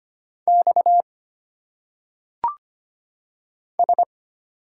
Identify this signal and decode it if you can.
Morse code, heard as a keyed tone, about 25 wpm, 700 Hz. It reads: X S
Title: Single Letters - Phonetic Pronunciation 25wpm